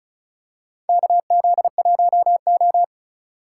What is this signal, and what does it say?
Morse code, heard as a keyed tone, about 36 wpm, 700 Hz. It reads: KZ1O
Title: USA Call Signs - 4 Characters - 4x Character Spacing - 35wpm